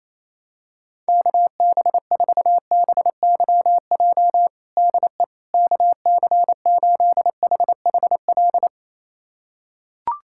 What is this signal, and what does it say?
Morse code, heard as a keyed tone, about 28 wpm, 700 Hz. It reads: KB4BYJ DE KC8HHL
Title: USA Callsign DE Callsign 28wpm